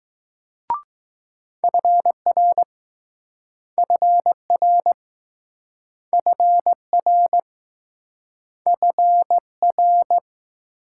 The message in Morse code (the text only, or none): FR FR FR FR